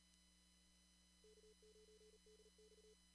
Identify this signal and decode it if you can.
Morse code, heard as a keyed tone, about 38 wpm, 430 Hz. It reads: K9DX
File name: CW / kt5j wpx cw-op-k5pi